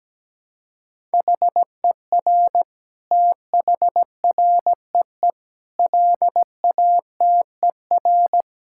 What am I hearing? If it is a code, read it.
Morse code, heard as a keyed tone, about 17 wpm, 700 Hz. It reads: HER THREE LATER
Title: Sentences from Top 1000 Words - Review Part 6 - 17wpm